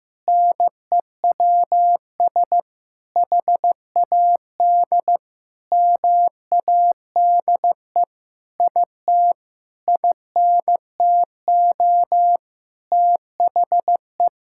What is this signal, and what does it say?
Morse code, heard as a keyed tone, about 15 wpm, 700 Hz. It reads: NEWS HAD MADE IT INTO THE
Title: Sentences from Top 1600 Words 15wpm